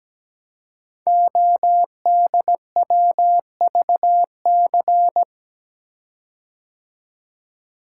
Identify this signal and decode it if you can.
Morse code, heard as a keyed tone, about 17 wpm, 700 Hz. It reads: ODWVC
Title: Code Groups - 5 Letters - 17wpm